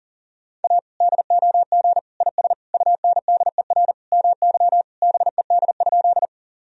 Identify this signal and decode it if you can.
Morse code, heard as a keyed tone, about 40 wpm, 700 Hz. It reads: A DOG IS UNDER MY BED?
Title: Sentences from Top 1000 Words - Review Part 2 - 40wpm